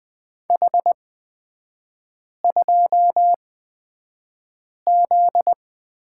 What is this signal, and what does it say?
Morse code, heard as a keyed tone, about 20 wpm, spaced as for 5 wpm, 700 Hz. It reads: H2Z
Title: 255 Lesson 41e - Period - Three Characters 20wpm